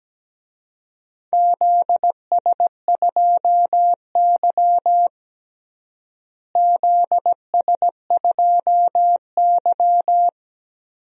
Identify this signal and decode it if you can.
Morse code, heard as a keyed tone, about 17 wpm, 700 Hz. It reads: ZS2Y ZS2Y
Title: DXCC-100 Call Signs - 4 Characters - Repeated 2x - 17wpm